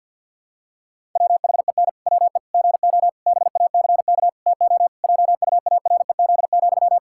Morse code, heard as a keyed tone, about 50 wpm, 700 Hz. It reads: WHEN WE GO BACK TO JUAREZ,
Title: Pangram Challenge v2 - 50wpm